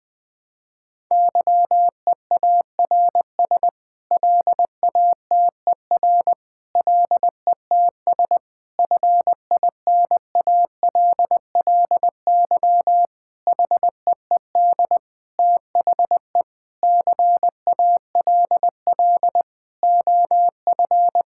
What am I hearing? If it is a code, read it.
Morse code, heard as a keyed tone, about 20 wpm, 700 Hz. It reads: YEARS LATER LETS FINALLY HEED THE CALL OF
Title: Sets of 8 Words - 20wpm